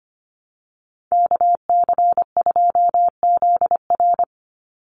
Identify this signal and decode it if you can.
Morse code, heard as a keyed tone, about 25 wpm, 700 Hz. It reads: KC2ZR
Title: USA Call Signs 25wpm